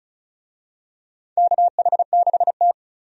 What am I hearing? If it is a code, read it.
Morse code, heard as a keyed tone, about 34 wpm, 700 Hz. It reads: KH6T
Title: USA Call Signs - 4 Characters - 35wpm